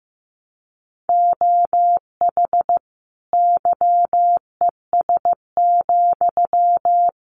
Morse code, heard as a keyed tone, about 14 wpm, 700 Hz. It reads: OH YES,
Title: Sentences from Top 1000 Words 15wpm